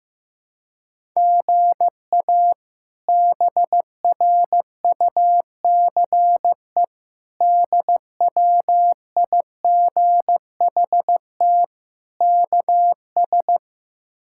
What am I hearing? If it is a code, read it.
Morse code, heard as a keyed tone, about 16 wpm, 700 Hz. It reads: GA BRUCE DWIGHT KS
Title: K1USN SST QSO Practice - 15wpm